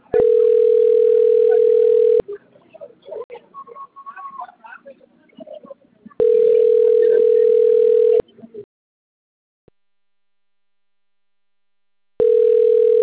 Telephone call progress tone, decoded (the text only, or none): ringback tone